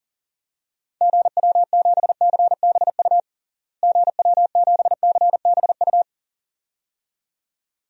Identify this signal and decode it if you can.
Morse code, heard as a keyed tone, about 40 wpm, 700 Hz. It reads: GW7CBU GW7CBU
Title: DXCC-100 Call Signs - 6 Characters - Repeated 2x - 40wpm